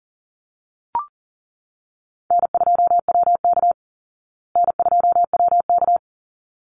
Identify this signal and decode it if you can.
Morse code, heard as a keyed tone, about 40 wpm, 700 Hz. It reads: N2WK N2WK
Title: Easy Call Signs Repeated 2x - 40wpm